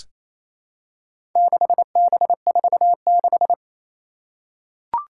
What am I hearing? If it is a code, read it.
Morse code, heard as a keyed tone, about 28 wpm, 700 Hz. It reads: 6B46
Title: Four Challenging Letter-Number Combinations 28wpm